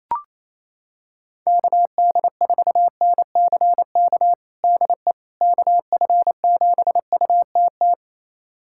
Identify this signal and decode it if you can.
Morse code, heard as a keyed tone, about 28 wpm, 700 Hz. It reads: KD4NCK DE KF7UTT
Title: USA Callsign DE Callsign 28wpm